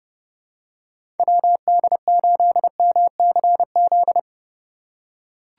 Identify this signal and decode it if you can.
Morse code, heard as a keyed tone, about 30 wpm, 700 Hz. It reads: WD8MCZ
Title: USA Call Signs - 6 Characters - 3x Character Spacing - 30wpm